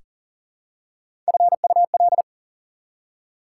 Morse code, heard as a keyed tone, about 40 wpm, 700 Hz. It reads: FUL
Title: Prefixes and Suffixes 40wpm